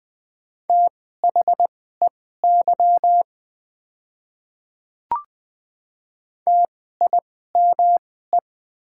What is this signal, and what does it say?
Morse code, heard as a keyed tone, about 20 wpm, spaced as for 14 wpm, 700 Hz. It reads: THEY TIME
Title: 2x Character Spacing - Top 100 Words - 20wpm